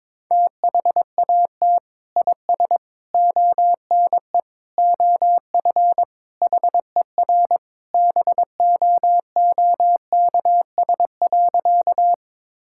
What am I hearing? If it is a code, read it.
Morse code, heard as a keyed tone, about 22 wpm, 700 Hz. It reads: THAT IS ONE OF HER BOOKS.
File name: practice / 22wpm / Sentences with 2-Word Phrases - 22wpm